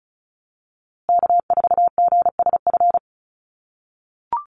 Morse code, heard as a keyed tone, about 34 wpm, 700 Hz. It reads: K4GSF
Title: USA Call Signs Repeated 2x 35wpm